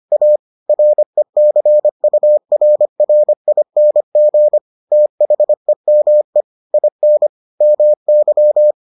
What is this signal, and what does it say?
Morse code, heard as a keyed tone, about 25 wpm, 600 Hz. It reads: A RECURRING THEME IN MY